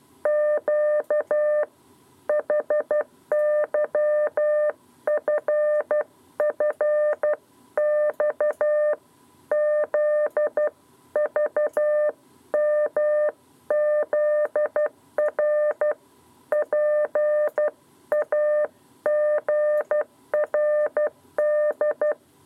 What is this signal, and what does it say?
Morse code, heard as a keyed tone, about 12 wpm, 600 Hz. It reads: Q HYFFX ZVMZR PAGRD
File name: 7b lezione cw 2